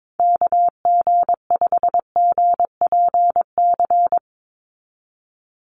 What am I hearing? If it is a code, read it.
Morse code, heard as a keyed tone, about 22 wpm, 700 Hz. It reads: KG5GPC
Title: USA Call Signs 22wpm